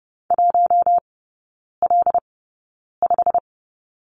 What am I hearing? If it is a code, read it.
Morse code, heard as a keyed tone, about 30 wpm, spaced as for 9 wpm, 700 Hz. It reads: 1L5
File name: course / 30:10wpm / 087 Lesson 15e - Number 5 - Three Characters 30wpm